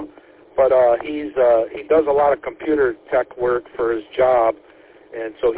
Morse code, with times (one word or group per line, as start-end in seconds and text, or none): none